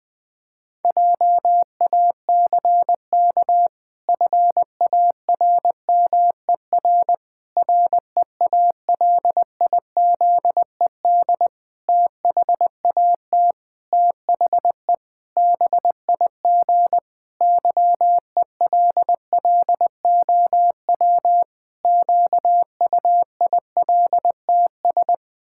Morse code, heard as a keyed tone, about 20 wpm, 700 Hz. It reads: JACK FARMER REALIZED THAT THE BIG YELLOW QUILTS
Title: Pangram Challenge v2 - 20wpm